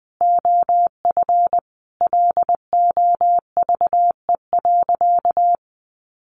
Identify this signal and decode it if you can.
Morse code, heard as a keyed tone, about 20 wpm, 700 Hz. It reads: OF LOVE.
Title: Sentences from Top 1300 Words 20wpm